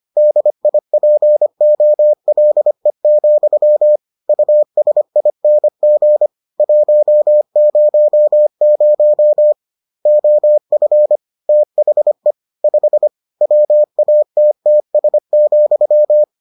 Morse code, heard as a keyed tone, about 25 wpm, 600 Hz. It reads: DIPOLE, USING 100 OF THE 5 WATTS,